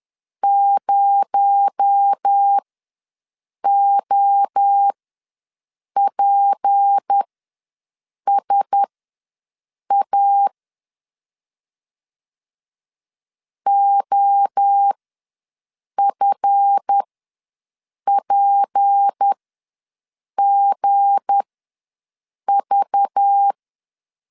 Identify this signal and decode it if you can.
Morse code, heard as a keyed tone, about 10 wpm, spaced as for 6 wpm, 800 Hz. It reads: 0OPSA OFPGV
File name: LEZIONI / 7 GF9Z / 7 GF ws 10wpm 3918